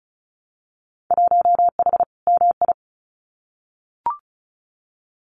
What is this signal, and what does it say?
Morse code, heard as a keyed tone, about 36 wpm, 700 Hz. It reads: 1H MI
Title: Winter Field Day Exchanges 35wpm